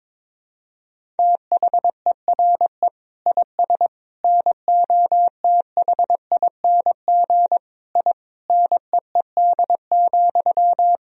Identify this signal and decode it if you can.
Morse code, heard as a keyed tone, about 22 wpm, 700 Hz. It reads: THERE IS NOTHING I NEED,
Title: Sentences from Top 1000 Words - Review Part 1 - 22wpm